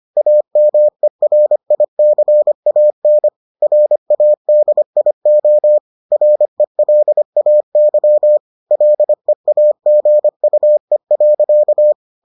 Morse code, heard as a keyed tone, about 25 wpm, 600 Hz. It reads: AMERICAN RADIO RELAY LEAGUE.